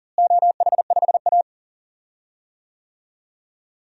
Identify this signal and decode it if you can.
Morse code, heard as a keyed tone, about 40 wpm, 700 Hz. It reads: OH5A
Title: DXCC-100 Call Signs - 4 Characters - Repeated 3x - 40wpm